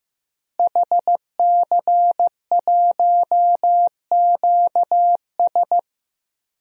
Morse code, heard as a keyed tone, about 15 wpm, 700 Hz. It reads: HC1QS
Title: DXCC-100 Call Signs - 5 Characters - Repeated 3x - 15wpm